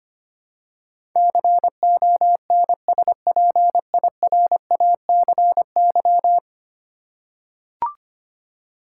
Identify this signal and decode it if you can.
Morse code, heard as a keyed tone, about 25 wpm, 700 Hz. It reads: CONSPIRACY
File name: 10 Letter Words - First 3 Letters Given - 25wpm